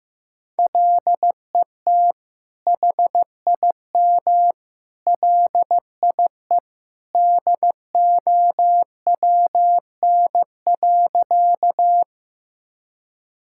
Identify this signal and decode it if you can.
Morse code, heard as a keyed tone, about 15 wpm, 700 Hz. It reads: LET HIM LIE DOWN.
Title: Sentences from Top 1600 Words 15wpm